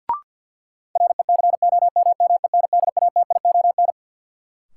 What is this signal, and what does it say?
Morse code, heard as a keyed tone, about 50 wpm, 700 Hz. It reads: RECOMMENDATION
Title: Sentences from Top 1400 Words - New Word Drill Alt - 50wpm